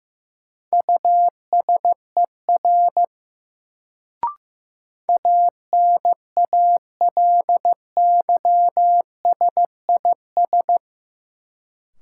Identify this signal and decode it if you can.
Morse code, heard as a keyed tone, about 15 wpm, 700 Hz. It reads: USER ANALYSIS
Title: Sentences from Top 1000 Words - Review Part 5 - New Word Drill - 15wpm